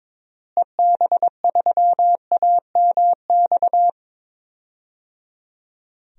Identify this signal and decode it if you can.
Morse code, heard as a keyed tone, about 21 wpm, 700 Hz. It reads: EB3AMX
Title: DXCC-100 Call Signs - 6 Characters - Repeated 3x - 22wpm